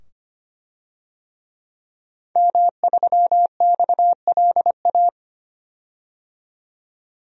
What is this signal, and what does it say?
Morse code, heard as a keyed tone, about 25 wpm, 700 Hz. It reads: M3XLA
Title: DXCC-100 Call Signs - 5 Characters - 25wpm